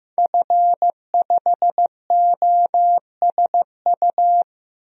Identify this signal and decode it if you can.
Morse code, heard as a keyed tone, about 15 wpm, 700 Hz. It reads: F5OSU
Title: DXCC-100 Call Signs - 5 Characters - Repeated 2x - 15wpm